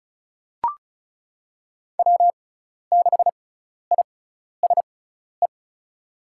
Morse code, heard as a keyed tone, about 36 wpm, spaced as for 12 wpm, 700 Hz. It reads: W6ISE
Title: USA Call Signs - 5 Characters - 6x Character Spacing - 35wpm